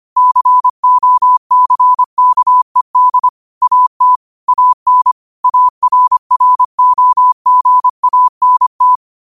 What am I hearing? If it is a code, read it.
Morse code, heard as a keyed tone, about 25 wpm, 1000 Hz. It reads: COCKED AT AN ARROGANT
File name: qotdfiles / 25wpm